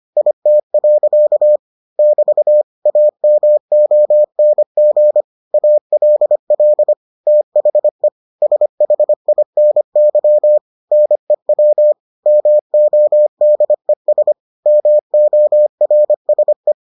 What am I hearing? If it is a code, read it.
Morse code, heard as a keyed tone, about 25 wpm, 600 Hz. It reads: IT. <BT> AMONG ALL THE SHINY NEW MODES MORSE